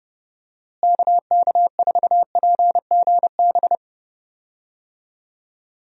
Morse code, heard as a keyed tone, about 31 wpm, 700 Hz. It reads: KK4PGB
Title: USA Call Signs - 6 Characters - Repeated 2x - 30wpm